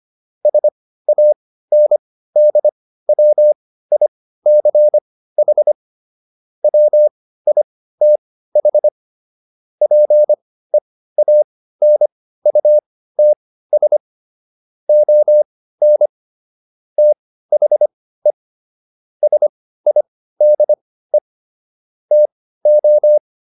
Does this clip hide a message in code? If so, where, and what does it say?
Yes: Morse code from 0.5 to 23.2 s, heard as a keyed tone, about 25 wpm, spaced as for 15 wpm, 600 Hz. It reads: SANDWICH WITH PEANUTS ON THE SIDE TO